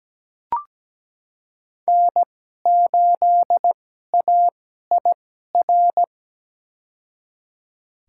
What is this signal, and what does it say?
Morse code, heard as a keyed tone, about 17 wpm, spaced as for 12 wpm, 700 Hz. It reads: N8AIR
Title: USA Call Signs - 5 Characters - 2x Character Spacing - 17wpm